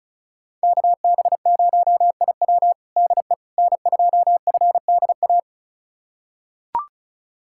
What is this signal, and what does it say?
Morse code, heard as a keyed tone, about 34 wpm, 700 Hz. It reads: KB0IW DE N2FDA
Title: USA Callsign DE Callsign 35wpm